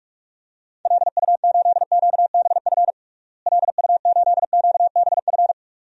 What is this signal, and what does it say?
Morse code, heard as a keyed tone, about 46 wpm, 700 Hz. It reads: LU8QBF LU8QBF
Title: DXCC-100 Call Signs - 6 Characters - Repeated 2x - 45wpm